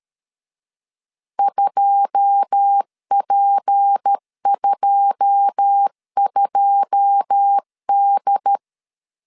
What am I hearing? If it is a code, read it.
Morse code, heard as a keyed tone, about 13 wpm, 800 Hz. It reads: 2P22D
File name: LEZIONI / 9 2 12wpm 2301